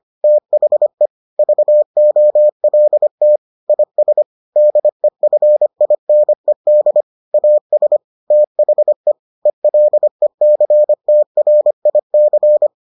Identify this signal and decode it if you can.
Morse code, heard as a keyed tone, about 25 wpm, 600 Hz. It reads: THE VOLT IS DEFINED AS THE ELECTRIC